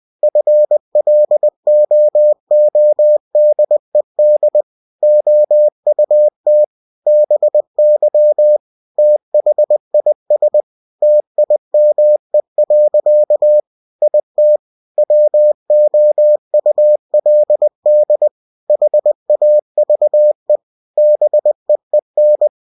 Morse code, heard as a keyed tone, about 20 wpm, 600 Hz. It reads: FLOODED OUT BY THIS TIME. IT WOULD HAVE BEEN
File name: cw-books / winnie-the-pooh / winnie-20wpm-page-0091